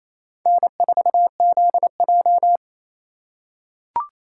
Morse code, heard as a keyed tone, about 28 wpm, 700 Hz. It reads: N4ZJ
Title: USA Call Signs - 4 Characters - 3x Character Spacing - 28wpm